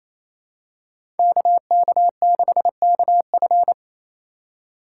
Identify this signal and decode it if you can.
Morse code, heard as a keyed tone, about 28 wpm, 700 Hz. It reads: KK6KF
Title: USA Call Signs - 5 Characters - 8x Character Spacing - 28wpm